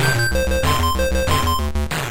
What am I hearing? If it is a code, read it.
There is no code.